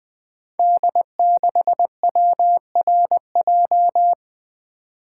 Morse code, heard as a keyed tone, about 20 wpm, 700 Hz. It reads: D6WRJ
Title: Code Groups - 5 Letter:Numbers - 20wpm